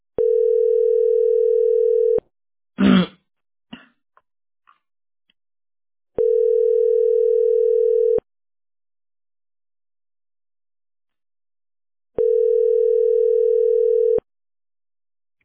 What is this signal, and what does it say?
Ringback tone (North America), 440 + 480 Hz.